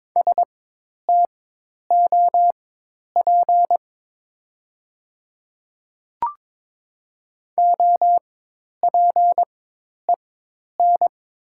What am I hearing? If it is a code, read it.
Morse code, heard as a keyed tone, about 22 wpm, spaced as for 10 wpm, 700 Hz. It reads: STOP OPEN
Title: 4x Character Spacing - Top 300 Words - 22wpm